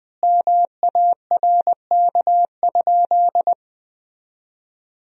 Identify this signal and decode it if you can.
Morse code, heard as a keyed tone, about 20 wpm, 700 Hz. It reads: MARK?
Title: 3x Word-Spacing - Sentences from Top 300 Words - 20wpm